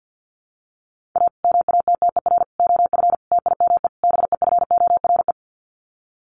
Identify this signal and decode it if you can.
Morse code, heard as a keyed tone, about 50 wpm, 700 Hz. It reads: A MATTER OF TIME BEFORE
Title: Sets of 5 Words Encore 50wpm